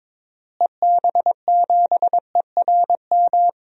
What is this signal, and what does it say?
Morse code, heard as a keyed tone, about 22 wpm, 700 Hz. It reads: EB7ERM
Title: DXCC-100 Call Signs - 6 Characters - Repeated 2x - 22wpm